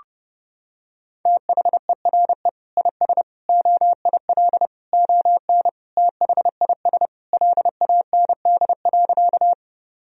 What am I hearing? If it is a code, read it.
Morse code, heard as a keyed tone, about 30 wpm, 700 Hz. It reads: THERE IS OIL ON THIS LAND.
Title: Sentences from Top 1000 Words - Review Part 4 - 30wpm